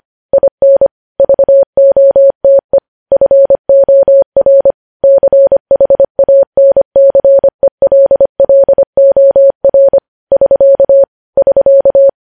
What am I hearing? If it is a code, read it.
Morse code, heard as a keyed tone, about 25 wpm, 550 Hz. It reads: IN VOTE FOR CHANCELLOR <SK> <SK>
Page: news-headlines-in-morse-code-at-25-wpm